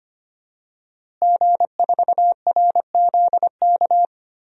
Morse code, heard as a keyed tone, about 25 wpm, 700 Hz. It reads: G4RZK
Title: DXCC-100 Call Signs - 5 Characters - Repeated 3x - 25wpm